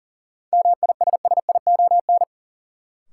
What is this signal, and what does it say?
Morse code, heard as a keyed tone, about 40 wpm, 700 Hz. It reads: MISSION
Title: Sentences from Top 1600 Words - New Word Drill - 40wpm